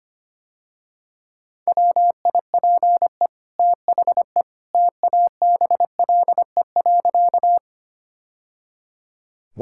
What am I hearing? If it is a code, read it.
Morse code, heard as a keyed tone, about 25 wpm, 700 Hz. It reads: WIPE THE TABLE.
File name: Easy Sentences with Hints - 25wpm